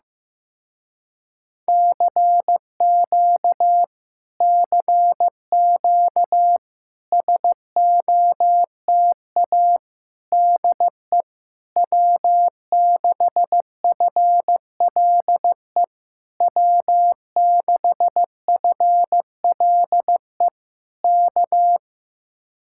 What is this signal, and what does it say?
Morse code, heard as a keyed tone, about 15 wpm, 700 Hz. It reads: CQ CQ SOTA DE W6FLE W6FLE K
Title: Speed-Racing Common QSO Phrases Encore - 15wpm